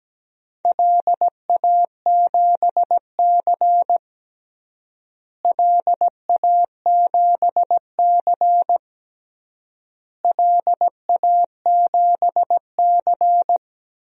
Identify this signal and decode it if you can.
Morse code, heard as a keyed tone, about 17 wpm, 700 Hz. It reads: LA7C LA7C LA7C
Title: DXCC-100 Call Signs - 4 Characters - Repeated 3x - 17wpm